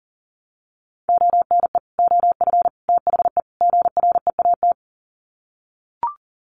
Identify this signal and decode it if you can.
Morse code, heard as a keyed tone, about 40 wpm, 700 Hz. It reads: ONE OF THE GREAT
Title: Sets of 4 Words 40wpm